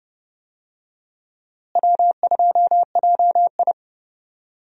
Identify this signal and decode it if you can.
Morse code, heard as a keyed tone, about 30 wpm, 700 Hz. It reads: W2JI
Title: USA Call Signs - 4 Characters - Repeated 2x - 30wpm